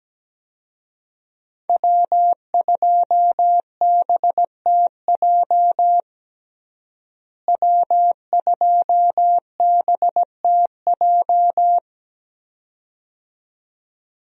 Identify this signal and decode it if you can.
Morse code, heard as a keyed tone, about 17 wpm, 700 Hz. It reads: W2BTJ W2BTJ
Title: USA Call Signs - 5 Characters - Repeated 2x - 17wpm